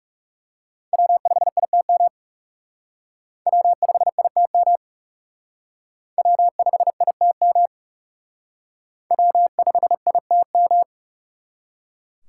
Morse code, heard as a keyed tone, about 37 wpm, 700 Hz. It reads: W5ITM W5ITM W5ITM W5ITM
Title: Speed Racing - USA 1x3 Call Signs 30wpm